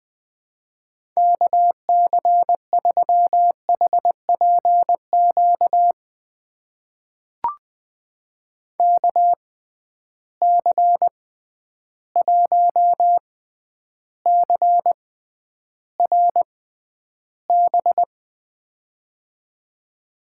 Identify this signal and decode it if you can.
Morse code, heard as a keyed tone, about 20 wpm, spaced as for 10 wpm, 700 Hz. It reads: KC3HPQ KC1CRB
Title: USA Call Signs - 6 Characters - 6x Character Spacing - 20wpm